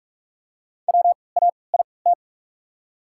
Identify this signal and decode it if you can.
Morse code, heard as a keyed tone, about 46 wpm, spaced as for 26 wpm, 700 Hz. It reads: WAIT
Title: 3x Character Spacing - Top 500 Words - 45wpm